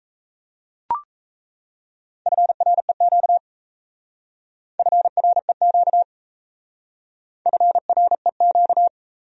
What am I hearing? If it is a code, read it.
Morse code, heard as a keyed tone, about 34 wpm, 700 Hz. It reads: FREQ FREQ FREQ
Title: Speed-Racing Abbreviations - 28wpm